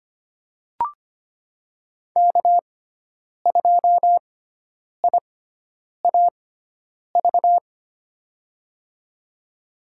Morse code, heard as a keyed tone, about 25 wpm, spaced as for 9 wpm, 700 Hz. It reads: K2IAV
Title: USA Call Signs - 6x Character Spacing - 25wpm